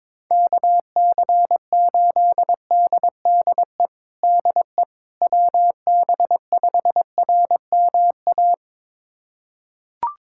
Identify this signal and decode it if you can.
Morse code, heard as a keyed tone, about 22 wpm, 700 Hz. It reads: KC8DDE DE WB5RMA
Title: USA Callsign DE Callsign - 6 Characters - 22wpm